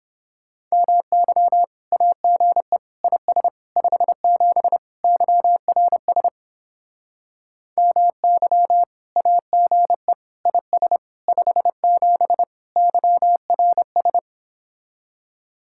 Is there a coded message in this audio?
Yes: Morse code from 0.7 to 14.2 s, heard as a keyed tone, about 31 wpm, 700 Hz. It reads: MY AGE IS 57 YRS MY AGE IS 57 YRS